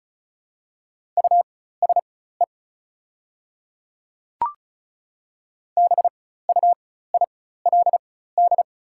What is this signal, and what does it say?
Morse code, heard as a keyed tone, about 34 wpm, spaced as for 16 wpm, 690 Hz. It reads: USE BUILD
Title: 4x Character Spacing - Top 200 Words - 35wpm